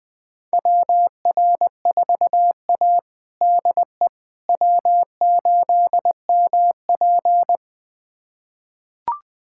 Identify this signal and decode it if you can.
Morse code, heard as a keyed tone, about 21 wpm, 700 Hz. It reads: WR4A DE W8MP
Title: USA Callsign DE Callsign - 4 Characters - 20wpm